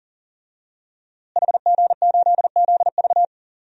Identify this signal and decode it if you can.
Morse code, heard as a keyed tone, about 40 wpm, 700 Hz. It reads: HG8ZV
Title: DXCC-100 Call Signs - 5 Characters - Repeated 3x - 40wpm